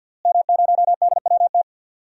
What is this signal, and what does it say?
Morse code, heard as a keyed tone, about 50 wpm, 700 Hz. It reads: M0DWT
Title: DXCC-100 Call Signs - 5 Characters - Repeated 3x - 50wpm